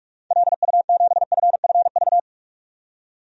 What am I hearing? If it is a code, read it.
Morse code, heard as a keyed tone, about 45 wpm, 700 Hz. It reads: LU7FFV
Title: DXCC-100 Call Signs - 6 Characters - 45wpm